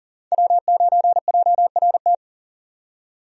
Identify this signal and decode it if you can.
Morse code, heard as a keyed tone, about 40 wpm, 700 Hz. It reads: W9JRT
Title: USA Call Signs - 5 Characters - 40wpm